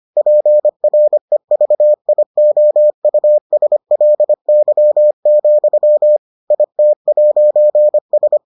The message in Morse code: PREVIOUSLY, IT'S

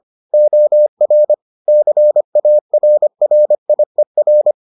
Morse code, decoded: OR CARRIER

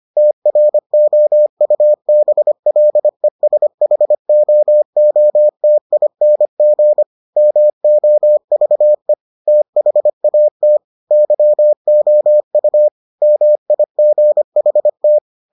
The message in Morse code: TROUBLESHOOTING MOVE THAT YOU MIGHT

